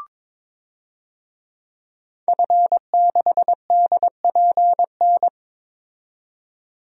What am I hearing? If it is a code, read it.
Morse code, heard as a keyed tone, about 22 wpm, 700 Hz. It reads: F6DPN
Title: DXCC-100 Call Signs - 5 Characters - 22wpm